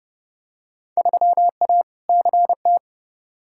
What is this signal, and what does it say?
Morse code, heard as a keyed tone, about 30 wpm, 700 Hz. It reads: 3A CT
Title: Field Day Exchanges - 30wpm